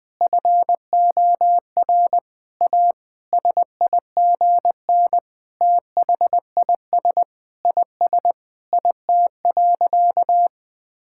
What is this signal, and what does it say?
Morse code, heard as a keyed tone, about 20 wpm, 700 Hz. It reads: FOR A SIGN THIS IS IT.
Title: Sentences with 6-Word Phrases - 20wpm